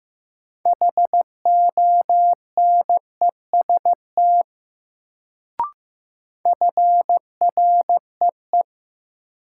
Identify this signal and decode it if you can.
Morse code, heard as a keyed tone, about 15 wpm, 700 Hz. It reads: HONEST FREE
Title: Sets of 5 Words Encore - New Word Drill - 15wpm